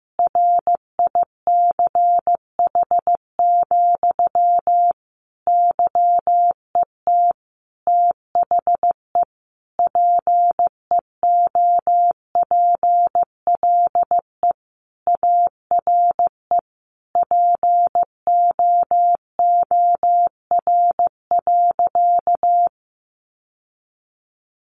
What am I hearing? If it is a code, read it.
Morse code, heard as a keyed tone, about 15 wpm, 700 Hz. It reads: RICH, YET THE PEOPLE ARE POOR.
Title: Sentences from Top 1100 Words 15wpm